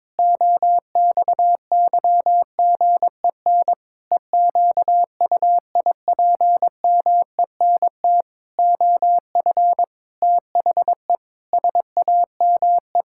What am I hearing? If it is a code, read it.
Morse code, heard as a keyed tone, about 22 wpm, 700 Hz. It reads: OXYGEN EQUIPMENT OF THE SAME
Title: Pangram Challenge v2 - 22wpm